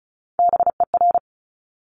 Morse code, heard as a keyed tone, about 36 wpm, 700 Hz. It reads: BER
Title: Top Three Letter Combinations in Words 35wpm